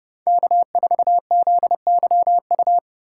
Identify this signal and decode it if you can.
Morse code, heard as a keyed tone, about 30 wpm, 700 Hz. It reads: K4ZYU
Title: USA Call Signs - 5 Characters - 3x Character Spacing - 30wpm